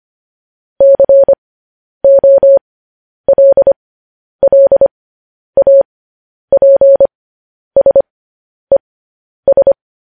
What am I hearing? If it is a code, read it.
Morse code, heard as a keyed tone, about 25 wpm, spaced as for 10 wpm, 550 Hz. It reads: COLLAPSES
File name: podcast / News headlines-1x-25x10